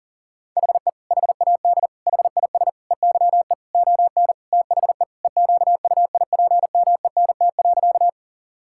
Morse code, heard as a keyed tone, about 37 wpm, 700 Hz. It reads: HE HAD HIS EYE ON THE EQUIPMENT.
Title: Sentences from Top 1000 Words - Review Part 5 - 40wpm